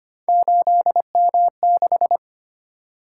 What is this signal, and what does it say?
Morse code, heard as a keyed tone, about 25 wpm, 700 Hz. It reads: 8M6
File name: Three Letter-Number Combinations - 25wpm